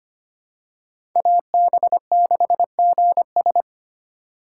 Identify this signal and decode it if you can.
Morse code, heard as a keyed tone, about 25 wpm, 700 Hz. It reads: AB6GS
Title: USA Call Signs - 5 Characters - 3x Character Spacing - 25wpm